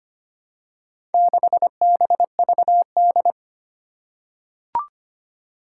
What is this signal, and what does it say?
Morse code, heard as a keyed tone, about 25 wpm, 700 Hz. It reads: 6BVD
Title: Four Challenging Letter-Number Combinations 25wpm